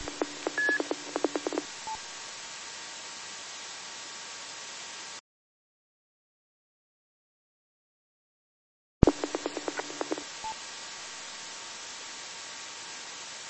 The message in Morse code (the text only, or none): none